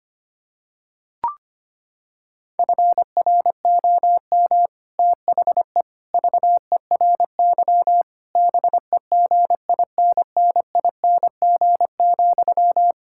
Morse code, heard as a keyed tone, about 25 wpm, 700 Hz. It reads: FROM THE VERY BEGINNING,
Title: Sentences from Top 1600 Words 25wpm